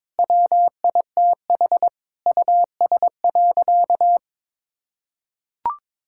Morse code, heard as a keyed tone, about 22 wpm, 700 Hz. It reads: WITH US.
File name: practice / 22wpm / Sentences with 8-Word Phrases - 22wpm